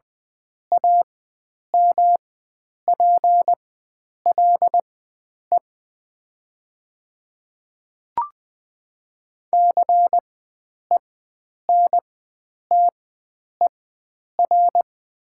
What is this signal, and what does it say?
Morse code, heard as a keyed tone, about 20 wpm, spaced as for 9 wpm, 700 Hz. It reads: AMPLE CENTER
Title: 4x Character Spacing - Top 500 Words - 20wpm